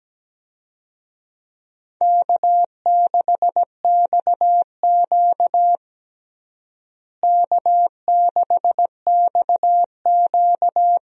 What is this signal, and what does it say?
Morse code, heard as a keyed tone, about 17 wpm, 700 Hz. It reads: K6XQ K6XQ
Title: USA Call Signs - 4 Characters - Repeated 3x - 17wpm